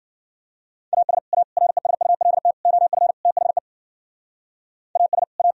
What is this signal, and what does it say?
Morse code, heard as a keyed tone, about 53 wpm, 700 Hz. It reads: AS A RESULT OF THE AS A